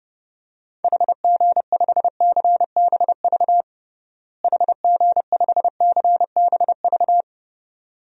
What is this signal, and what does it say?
Morse code, heard as a keyed tone, about 30 wpm, 700 Hz. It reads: HG5CBV HG5CBV
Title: DXCC-100 Call Signs - 6 Characters - Repeated 2x - 30wpm